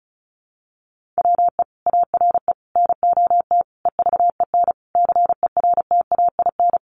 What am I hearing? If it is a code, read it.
Morse code, heard as a keyed tone, about 34 wpm, 700 Hz. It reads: WE ARE NOT EVEN CERTAIN
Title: Sentences from Top 1500 Words 35wpm